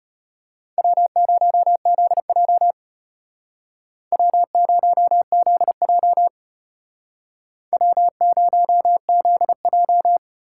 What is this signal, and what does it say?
Morse code, heard as a keyed tone, about 32 wpm, 700 Hz. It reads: W0ZJ W0ZJ W0ZJ